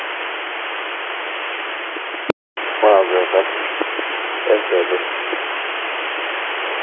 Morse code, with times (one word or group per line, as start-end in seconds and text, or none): none